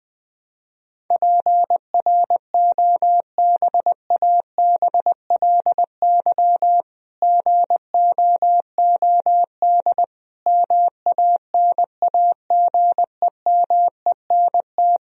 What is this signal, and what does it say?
Morse code, heard as a keyed tone, about 21 wpm, 700 Hz. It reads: PROBABLY GOOD MANAGEMENT
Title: Sentences from Top 1000 Words - Review Part 4 - 20wpm